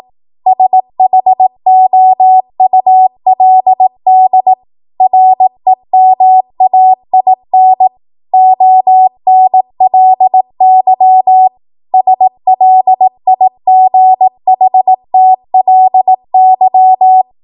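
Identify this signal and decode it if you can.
Morse code, heard as a keyed tone, about 18 wpm, 750 Hz. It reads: SHOULD REMAIN ONLY SLIGHTLY
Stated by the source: ARRL 18 wpm Morse code practice archive transmission.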